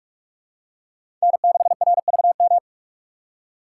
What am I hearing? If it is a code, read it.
Morse code, heard as a keyed tone, about 42 wpm, 700 Hz. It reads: N6RVM
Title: USA Call Signs - 2x Character Spacing - 45wpm